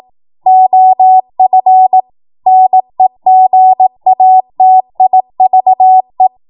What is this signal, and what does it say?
Morse code, heard as a keyed tone, about 18 wpm, 750 Hz. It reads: OF NEGATIVE
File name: Morse / 18WPM